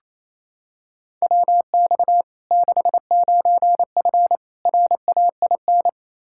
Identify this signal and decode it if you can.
Morse code, heard as a keyed tone, about 28 wpm, 700 Hz. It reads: WX 69F RAIN
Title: Practice QSO - Exchange 2 - Info - 28wpm